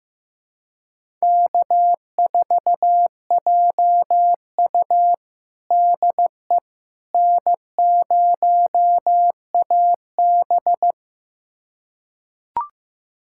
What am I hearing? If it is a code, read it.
Morse code, heard as a keyed tone, about 15 wpm, 700 Hz. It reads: K4JU DE N0AB
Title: USA Callsign DE Callsign - 4 Characters - 15wpm